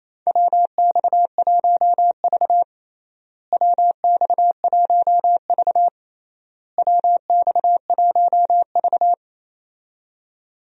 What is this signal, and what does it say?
Morse code, heard as a keyed tone, about 28 wpm, 700 Hz. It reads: WX1V WX1V WX1V
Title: USA Call Signs - 4 Characters - Repeated 3x - 28wpm